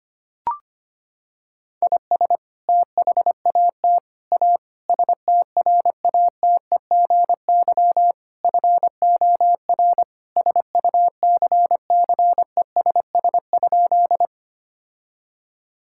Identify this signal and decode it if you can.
Morse code, heard as a keyed tone, about 25 wpm, 700 Hz. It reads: IS THAT A STRATEGY FOR SUCCESS?
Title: Sentences from Top 1000 Words - Review Part 5 - 25wpm